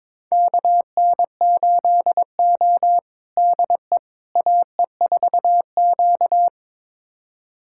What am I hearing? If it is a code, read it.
Morse code, heard as a keyed tone, about 21 wpm, 700 Hz. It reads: KN8O DE AE4Q
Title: USA Callsign DE Callsign - 4 Characters - 22wpm